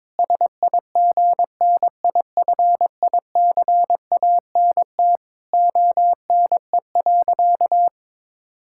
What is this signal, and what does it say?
Morse code, heard as a keyed tone, about 22 wpm, 700 Hz. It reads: SIGNIFICANT ONE.
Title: Sentences from Top 1000 Words - Review Part 6 - 22wpm